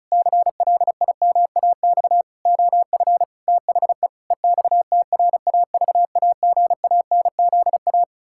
Morse code, heard as a keyed tone, about 34 wpm, 700 Hz. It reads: CLIMAX OF THE EXTRAVAGANZA
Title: Pangram Challenge v2 - 35wpm